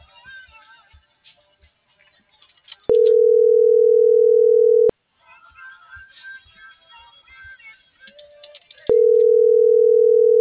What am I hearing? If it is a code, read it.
Ringback tone (North America), 440 + 480 Hz.
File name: CENTRAL SAGOT / out edgar.sagot